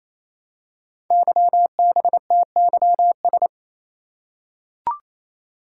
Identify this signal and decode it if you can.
Morse code, heard as a keyed tone, about 27 wpm, 700 Hz. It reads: YBTYS
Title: Code Groups - 5 Letters - 28wpm